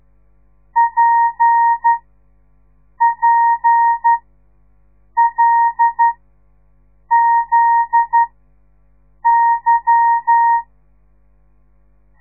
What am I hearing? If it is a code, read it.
Morse code, heard as a keyed tone, about 12 wpm, spaced as for 6 wpm, 930 Hz. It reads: PPLZY
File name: radiomorningstar / morse / les 8